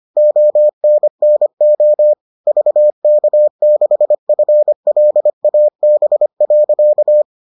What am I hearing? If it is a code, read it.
Morse code, heard as a keyed tone, about 25 wpm, 600 Hz. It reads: ONNO VK6FLAB.